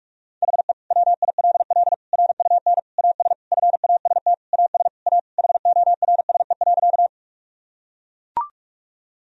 Morse code, heard as a keyed tone, about 41 wpm, 700 Hz. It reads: HE WILL RUN AS FAST AS A HORSE.
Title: Sentences from Top 1000 Words - Review Part 2 - 45wpm